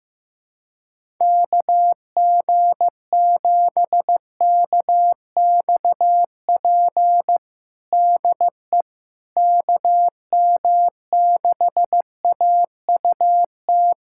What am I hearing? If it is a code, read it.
Morse code, heard as a keyed tone, about 15 wpm, 700 Hz. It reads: KG7KXP DE KM6AUT
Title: USA Callsign DE Callsign - 6 Characters - 15wpm